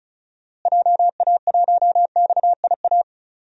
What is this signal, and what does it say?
Morse code, heard as a keyed tone, about 34 wpm, 700 Hz. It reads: JA1XIA
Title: DXCC-100 Call Signs - 6 Characters - Repeated 2x - 35wpm